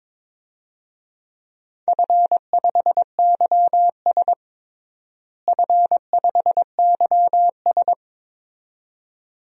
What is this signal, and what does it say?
Morse code, heard as a keyed tone, about 22 wpm, 700 Hz. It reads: F5YS F5YS
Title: DXCC-100 Call Signs - 4 Characters - Repeated 2x - 22wpm